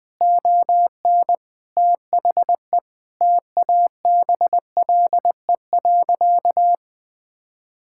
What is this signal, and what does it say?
Morse code, heard as a keyed tone, about 20 wpm, 700 Hz. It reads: ON THE TABLE.
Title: Sentences from Top 1000 Words - Review Part 5 - 20wpm